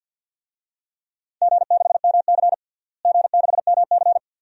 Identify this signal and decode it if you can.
Morse code, heard as a keyed tone, about 51 wpm, 700 Hz. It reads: G6MC G6MC
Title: DXCC-100 Call Signs - 4 Characters - Repeated 2x - 50wpm